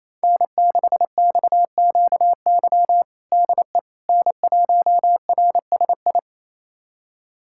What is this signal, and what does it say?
Morse code, heard as a keyed tone, about 28 wpm, 700 Hz. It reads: N6XQY DE N1RSI
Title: USA Callsign DE Callsign - 5 Characters - 28wpm